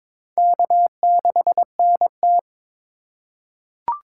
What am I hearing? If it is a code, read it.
Morse code, heard as a keyed tone, about 21 wpm, 700 Hz. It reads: K6NT
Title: USA Call Signs - 3x Character Spacing - 22wpm